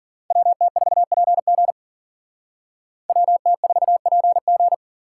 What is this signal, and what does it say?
Morse code, heard as a keyed tone, about 52 wpm, 700 Hz. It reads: WT4PG WT4PG